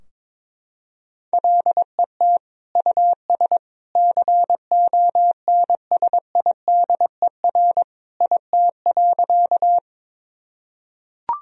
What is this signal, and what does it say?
Morse code, heard as a keyed tone, about 21 wpm, 700 Hz. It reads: LET US CONSIDER IT.
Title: Sentences from Top 1000 Words - Review Part 4 - 22wpm